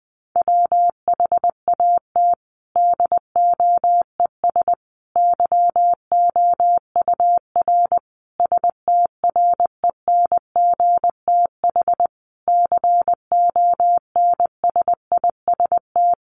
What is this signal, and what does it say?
Morse code, heard as a keyed tone, about 20 wpm, 700 Hz. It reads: WHAT DOES YOUR STRENGTH CONSIST
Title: Sentences from Top 1400 Words 20wpm